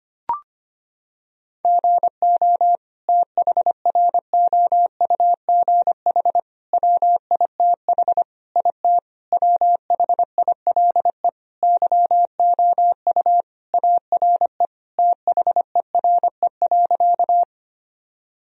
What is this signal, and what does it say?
Morse code, heard as a keyed tone, about 25 wpm, 700 Hz. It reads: GO THROUGH WITH IT WHILE YOU ARE THERE.
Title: Sentences from Top 1000 Words - Review Part 1 - 25wpm